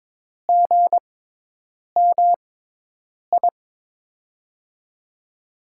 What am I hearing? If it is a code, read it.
Morse code, heard as a keyed tone, about 22 wpm, spaced as for 8 wpm, 700 Hz. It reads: GMI
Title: USA Call Signs - 5 Characters - 6x Character Spacing - 22wpm